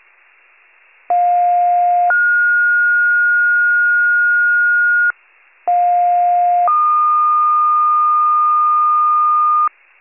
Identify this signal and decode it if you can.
Two-tone page, heard as a one-second tone, then a three-second tone, about 710 then 1430 Hz, about 710 then 1220 Hz, with noise. It is heard from 1.1 to 9.7 s.
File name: wcalert